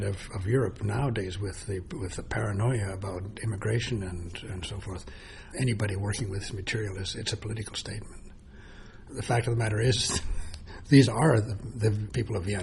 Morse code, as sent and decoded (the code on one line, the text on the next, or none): none